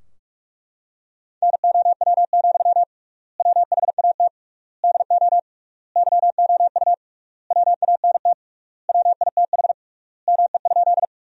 Morse code, heard as a keyed tone, about 42 wpm, 700 Hz. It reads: NOW, WHAT DO YOU WANT WITH ME?
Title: 3x Word-Spacing - Sentences from Top 200 Words - 45wpm